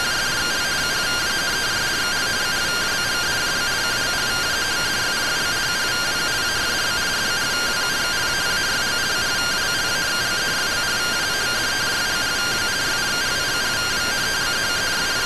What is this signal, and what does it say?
Two-tone teleprinter signal, about 170 Hz shift.